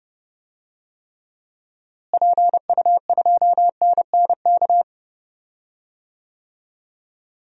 Morse code, heard as a keyed tone, about 31 wpm, 700 Hz. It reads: PU2NNK
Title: DXCC-100 Call Signs - 6 Characters - Repeated 2x - 30wpm